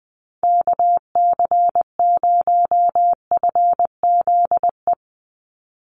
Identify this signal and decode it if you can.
Morse code, heard as a keyed tone, about 20 wpm, 700 Hz. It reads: KC0FZE
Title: USA Call Signs 20wpm